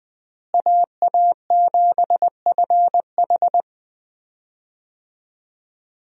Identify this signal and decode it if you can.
Morse code, heard as a keyed tone, about 20 wpm, 700 Hz. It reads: AA7FH
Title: USA Call Signs - 5 Characters - Repeated 3x - 20wpm